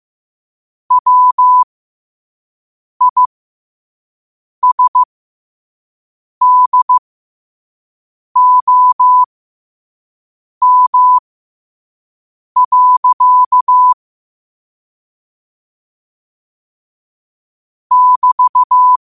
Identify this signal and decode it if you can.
Morse code, heard as a keyed tone, about 15 wpm, spaced as for 5 wpm, 1000 Hz. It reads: WISDOM. <BT>
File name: 5wpm